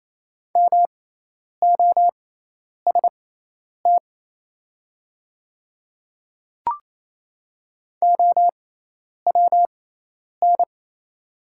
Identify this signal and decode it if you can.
Morse code, heard as a keyed tone, about 28 wpm, spaced as for 10 wpm, 700 Hz. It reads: MOST OWN
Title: 6x Character Spacing - Top 300 Words - 28wpm